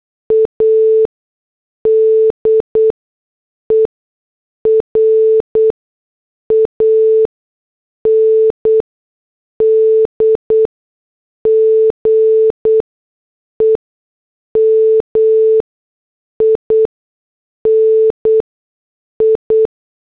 Morse code, heard as a keyed tone, about 8 wpm, 440 Hz. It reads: A D E R A N D G E M I N I
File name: morse / headlines